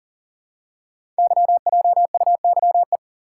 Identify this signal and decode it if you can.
Morse code, heard as a keyed tone, about 43 wpm, 700 Hz. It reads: YJUYE
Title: Code Groups - 5 Letters - 40wpm